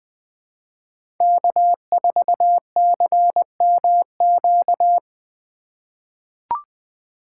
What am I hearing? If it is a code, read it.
Morse code, heard as a keyed tone, about 20 wpm, 700 Hz. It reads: K4CMQ
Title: USA Call Signs - 5 Characters - 4x Character Spacing - 20wpm